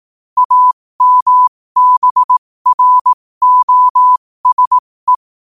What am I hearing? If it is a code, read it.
Morse code, heard as a keyed tone, about 18 wpm, spaced as for 15 wpm, 1000 Hz. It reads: AMBROSE
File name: qotdfiles / 15wpm